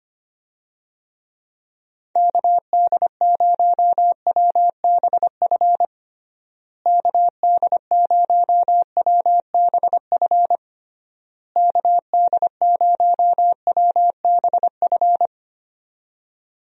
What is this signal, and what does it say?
Morse code, heard as a keyed tone, about 25 wpm, 700 Hz. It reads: KD0WBF KD0WBF KD0WBF
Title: USA Call Signs - 6 Characters - Repeated 3x - 25wpm